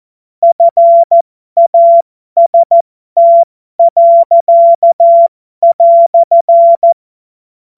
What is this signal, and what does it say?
Morse code, heard as a keyed tone, about 14 wpm, spaced as for 12 wpm, 680 Hz. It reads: FAST."
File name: Morsekurs / Witze